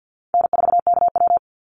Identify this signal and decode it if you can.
Morse code, heard as a keyed tone, about 50 wpm, 700 Hz. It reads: N4KW
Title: Call Signs - 50wpm